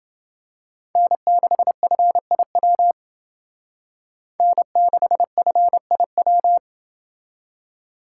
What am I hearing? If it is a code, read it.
Morse code, heard as a keyed tone, about 28 wpm, 700 Hz. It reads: N6FIW N6FIW